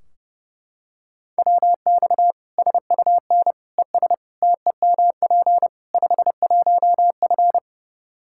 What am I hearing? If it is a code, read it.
Morse code, heard as a keyed tone, about 30 wpm, 700 Hz. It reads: WX SUN ES TEMP 51F
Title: Practice QSO - Exchange 2 - Info - 30wpm